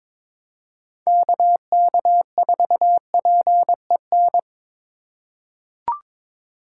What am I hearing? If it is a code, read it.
Morse code, heard as a keyed tone, about 22 wpm, 700 Hz. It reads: KK4PEN
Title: USA Call Signs - 6 Characters - 2x Character Spacing - 22wpm